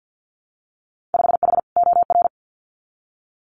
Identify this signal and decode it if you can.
Morse code, heard as a keyed tone, about 50 wpm, 700 Hz. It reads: 5H OR
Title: Winter Field Day Exchanges Encore 50wpm